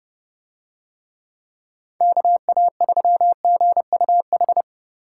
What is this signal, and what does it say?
Morse code, heard as a keyed tone, about 31 wpm, 700 Hz. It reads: KA3GUH